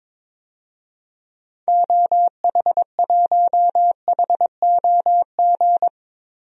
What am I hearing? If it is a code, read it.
Morse code, heard as a keyed tone, about 22 wpm, 700 Hz. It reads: OH1HOG
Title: DXCC-100 Call Signs - 6 Characters - Repeated 2x - 22wpm